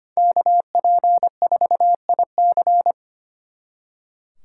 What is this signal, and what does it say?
Morse code, heard as a keyed tone, about 33 wpm, 700 Hz. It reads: KP4IC